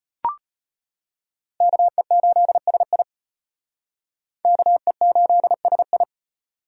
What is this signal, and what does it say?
Morse code, heard as a keyed tone, about 33 wpm, 700 Hz. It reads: KE8SI KE8SI